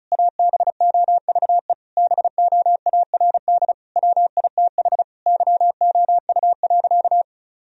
Morse code, heard as a keyed tone, about 34 wpm, 700 Hz. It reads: ABOVE BOARD WITH YOU.
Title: Sentences from Top 1000 Words - Review Part 4 - 35wpm